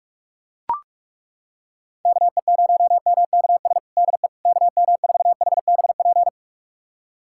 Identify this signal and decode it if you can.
Morse code, heard as a keyed tone, about 42 wpm, 700 Hz. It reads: KE0MKS DE KM4HBP
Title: USA Callsign DE Callsign - 6 Characters - 45wpm